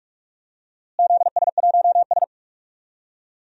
Morse code, heard as a keyed tone, about 43 wpm, 700 Hz. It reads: ZS1S
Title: DXCC-100 Call Signs - 4 Characters - Repeated 2x - 45wpm